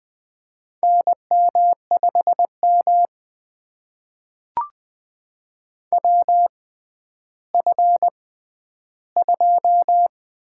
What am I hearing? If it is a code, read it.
Morse code, heard as a keyed tone, about 20 wpm, spaced as for 11 wpm, 700 Hz. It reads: NM5M WF2
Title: USA Call Signs - 4 Characters - 6x Character Spacing - 20wpm